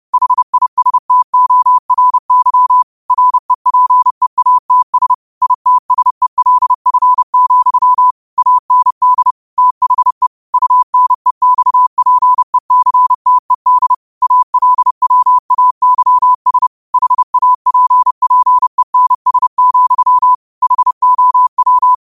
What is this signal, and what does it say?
Morse code, heard as a keyed tone, about 30 wpm, 1000 Hz. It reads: HISTORY REPEATS ITSELF, AND THE UNEXPECTED ALWAYS HAPPENS, HOW
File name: qotdfiles / 30wpm